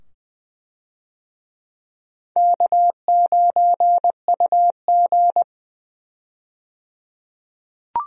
Morse code, heard as a keyed tone, about 21 wpm, 700 Hz. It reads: K9UG
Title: USA Call Signs - 4 Characters - 20wpm